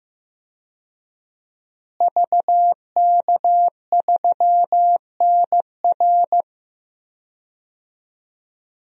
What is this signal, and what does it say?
Morse code, heard as a keyed tone, about 15 wpm, 700 Hz. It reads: VK3NR
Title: DXCC-100 Call Signs - 5 Characters - Repeated 3x - 15wpm